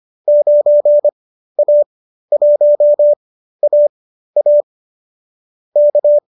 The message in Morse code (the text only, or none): 9A1AA K